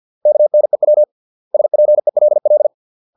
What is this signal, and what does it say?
Morse code, heard as a keyed tone, about 50 wpm, 600 Hz. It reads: KNEW SPELL